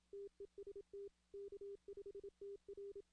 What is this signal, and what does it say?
Morse code, heard as a keyed tone, about 27 wpm, 400 Hz. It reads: TEST K5TR
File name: 2016 arrl dx cw / 2016-ARRL-DX-CW-K5TR- - 0-14